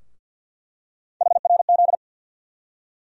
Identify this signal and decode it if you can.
Morse code, heard as a keyed tone, about 50 wpm, 700 Hz. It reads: HRZ
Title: USA Call Sign Postfixes - 3 Letters 50wpm